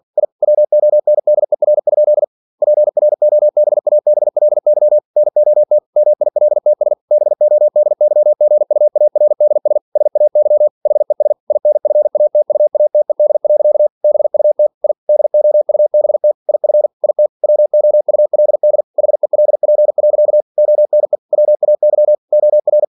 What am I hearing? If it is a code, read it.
Morse code, heard as a keyed tone, about 46 wpm, 600 Hz. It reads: I WONDER? PROBABLY NOT MILTS BODYGUARDS SAY HES INFATUATED. BUT I DOUBT IF IT WOULD HELP. ONE WAY OR
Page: Never 47wpm